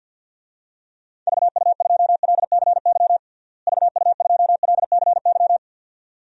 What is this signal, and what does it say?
Morse code, heard as a keyed tone, about 52 wpm, 700 Hz. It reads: VU2LCY VU2LCY
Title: DXCC-100 Call Signs - 6 Characters - Repeated 2x - 50wpm